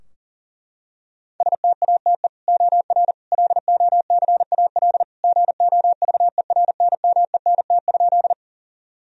Morse code, heard as a keyed tone, about 40 wpm, 700 Hz. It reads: STATE OR LOCAL GOVERNMENT?
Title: Sentences from Top 1000 Words - Review Part 3 - 40wpm